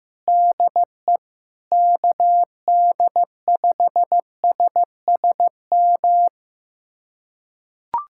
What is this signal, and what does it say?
Morse code, heard as a keyed tone, about 16 wpm, 700 Hz. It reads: DE KD5SSM
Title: USA Callsign DE Callsign - 6 Characters - 15wpm